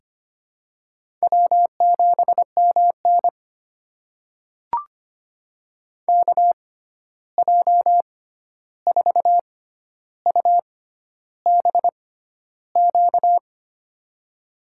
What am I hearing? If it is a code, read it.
Morse code, heard as a keyed tone, about 25 wpm, spaced as for 11 wpm, 700 Hz. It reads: W7MN KJ4UBQ
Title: USA Call Signs - 6x Character Spacing - 25wpm